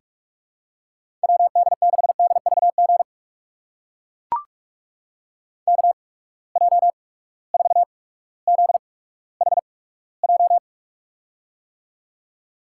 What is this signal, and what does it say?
Morse code, heard as a keyed tone, about 44 wpm, spaced as for 19 wpm, 700 Hz. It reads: WD6DVG KJ4ZHJ
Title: USA Call Signs - 6 Characters - 8x Character Spacing - 45wpm